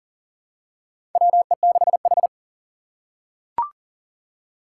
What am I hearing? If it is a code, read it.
Morse code, heard as a keyed tone, about 40 wpm, 700 Hz. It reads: WE6H
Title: USA Call Signs - 3x Character Spacing - 40wpm